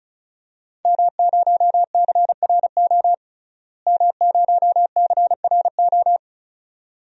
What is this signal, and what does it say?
Morse code, heard as a keyed tone, about 36 wpm, 700 Hz. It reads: M0CRO M0CRO
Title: DXCC-100 Call Signs - 5 Characters - Repeated 2x - 35wpm